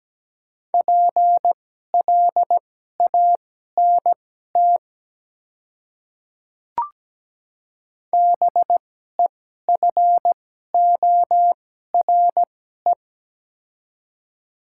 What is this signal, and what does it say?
Morse code, heard as a keyed tone, about 17 wpm, spaced as for 12 wpm, 700 Hz. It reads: PLANT BEFORE
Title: 2x Character Spacing - Top 300 Words - 17wpm